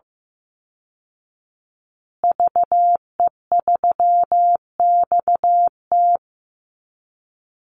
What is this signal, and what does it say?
Morse code, heard as a keyed tone, about 15 wpm, 700 Hz. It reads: VE3XT
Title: Call Signs Repeated 2x - 15wpm